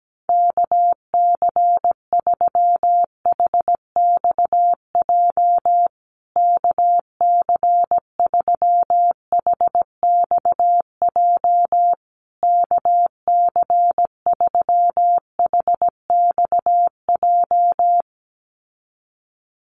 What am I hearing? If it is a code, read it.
Morse code, heard as a keyed tone, about 17 wpm, 700 Hz. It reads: KC3HXJ KC3HXJ KC3HXJ
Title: USA Call Signs Repeated 3x 17wpm